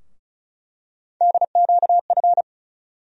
Morse code, heard as a keyed tone, about 34 wpm, 700 Hz. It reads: DQF
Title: USA Call Sign Postfixes - 3 Letters 35wpm